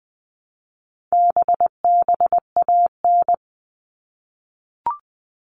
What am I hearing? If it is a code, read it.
Morse code, heard as a keyed tone, about 19 wpm, 700 Hz. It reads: BBAN
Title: All Cut Numbers - 4-Digit 20wpm